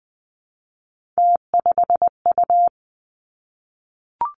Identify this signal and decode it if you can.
Morse code, heard as a keyed tone, about 20 wpm, 700 Hz. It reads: T5U
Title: Three Challenging Letter-Number Combinations 20wpm